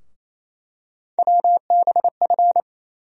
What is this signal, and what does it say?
Morse code, heard as a keyed tone, about 29 wpm, 700 Hz. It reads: WBF
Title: USA Call Sign Postfixes - 3 Letters 28wpm